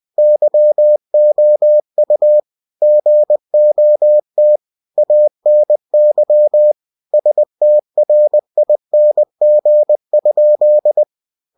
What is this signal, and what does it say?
Morse code, heard as a keyed tone, about 20 wpm, 600 Hz. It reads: YOU GOT ANY STRING?
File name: cw-books / winnie-the-pooh / winnie-20wpm-page-0038